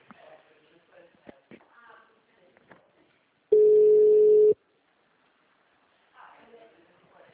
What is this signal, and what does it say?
Ringback tone (Europe, 425 Hz), with speech.